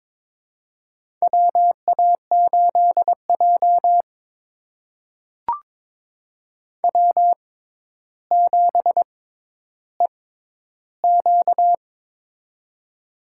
Morse code, heard as a keyed tone, about 22 wpm, spaced as for 11 wpm, 700 Hz. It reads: WA8J W7EQ
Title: USA Call Signs - 4 Characters - 6x Character Spacing - 22wpm